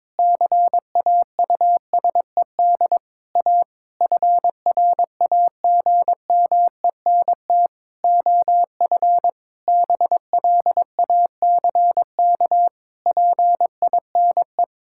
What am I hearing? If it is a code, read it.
Morse code, heard as a keyed tone, about 22 wpm, 700 Hz. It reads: CAUSED A FRAGMENT OF BLACK PINE
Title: Pangram Challenge v2 - 22wpm